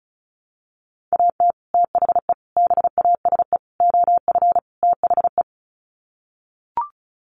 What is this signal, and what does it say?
Morse code, heard as a keyed tone, about 34 wpm, 700 Hz. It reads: AT THE BASE OF THE
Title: Sets of 5 Words 35wpm